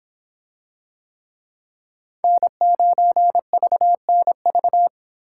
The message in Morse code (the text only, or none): N9VNV